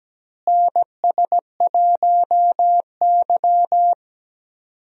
Morse code, heard as a keyed tone, about 17 wpm, 700 Hz. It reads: NS1Y
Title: USA Call Signs - 4 Characters - Repeated 3x - 17wpm